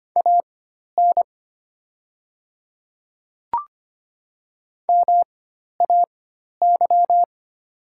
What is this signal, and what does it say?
Morse code, heard as a keyed tone, about 25 wpm, spaced as for 12 wpm, 700 Hz. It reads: AN MAY
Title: 4x Character Spacing - Top 100 Words - 25wpm